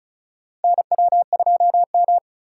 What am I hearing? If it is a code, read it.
Morse code, heard as a keyed tone, about 36 wpm, 700 Hz. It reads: NW2M
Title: USA Call Signs - 4x Character Spacing - 35wpm